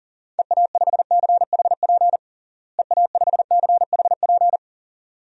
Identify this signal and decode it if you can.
Morse code, heard as a keyed tone, about 40 wpm, 700 Hz. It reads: EA5CHP EA5CHP
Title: DXCC-100 Call Signs - 6 Characters - Repeated 2x - 40wpm